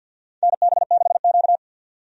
Morse code, heard as a keyed tone, about 50 wpm, 700 Hz. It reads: NB6Q
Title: USA Call Signs - 4 Characters - 50wpm